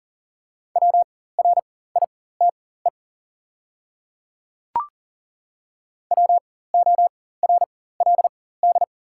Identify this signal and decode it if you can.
Morse code, heard as a keyed tone, about 40 wpm, spaced as for 19 wpm, 700 Hz. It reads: WRITE WORLD
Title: 4x Character Spacing - Top 200 Words - 40wpm